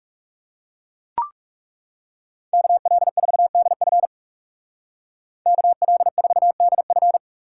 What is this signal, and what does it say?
Morse code, heard as a keyed tone, about 39 wpm, 700 Hz. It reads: KL4DF KL4DF